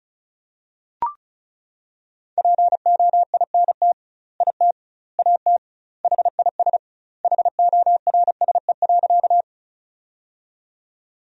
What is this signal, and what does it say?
Morse code, heard as a keyed tone, about 34 wpm, 700 Hz. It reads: POINT IT AT HIS HORSE.
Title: 2x Word-Spacing - Sentences from Top 300 Words - 35wpm